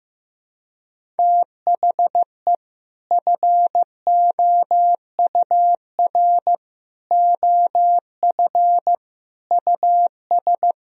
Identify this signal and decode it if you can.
Morse code, heard as a keyed tone, about 15 wpm, 700 Hz. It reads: THE FOUR OF US
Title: Sentences from Top 1000 Words - Review Part 1 - 15wpm